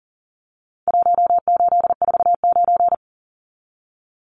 Morse code, heard as a keyed tone, about 40 wpm, 700 Hz. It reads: 1849
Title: Numbers - Contest Style - 4-Digit 40wpm